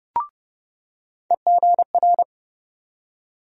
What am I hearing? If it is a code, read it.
Morse code, heard as a keyed tone, about 30 wpm, 700 Hz. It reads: EGR
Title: USA Call Sign Postfixes - 3 Letters 30wpm